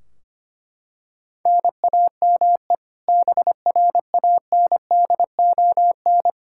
Morse code, heard as a keyed tone, about 25 wpm, 700 Hz. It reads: NAME BRANDON
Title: K1USN SST QSO Practice 25wpm